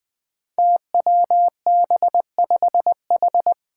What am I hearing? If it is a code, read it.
Morse code, heard as a keyed tone, about 20 wpm, 700 Hz. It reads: TWB5H
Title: Five Challenging Letter-Number Combinations 20wpm